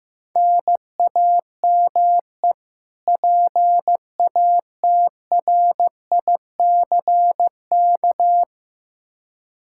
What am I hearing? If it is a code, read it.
Morse code, heard as a keyed tone, about 15 wpm, 700 Hz. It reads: NAME PATRICK
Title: K1USN SST QSO Practice 15wpm